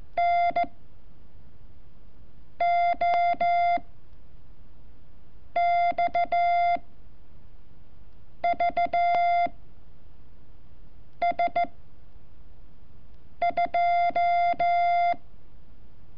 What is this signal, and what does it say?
Morse code, heard as a keyed tone, about 14 wpm, 690 Hz. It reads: N O X V S 2
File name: curso / ejercicio 29